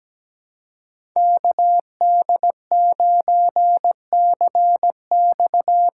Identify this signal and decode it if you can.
Morse code, heard as a keyed tone, about 17 wpm, 700 Hz. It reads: KD9CX
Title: USA Call Signs - 6 Characters - 6x Character Spacing - 17wpm